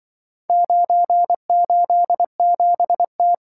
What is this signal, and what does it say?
Morse code, heard as a keyed tone, about 24 wpm, 700 Hz. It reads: 987T
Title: Speed-Racing - Four Letter-Number Combinations - 20wpm